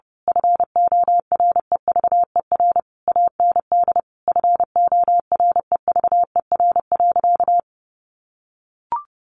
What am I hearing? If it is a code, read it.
Morse code, heard as a keyed tone, about 30 wpm, 700 Hz. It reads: FOREVER AND FOREVER.
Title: A Sound Of Thunder 30wpm